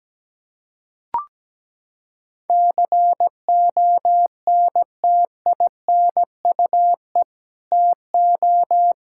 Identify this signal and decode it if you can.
Morse code, heard as a keyed tone, about 17 wpm, 700 Hz. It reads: CONTINUE TO
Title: Sentences from Top 1000 Words - Review Part 6 - 17wpm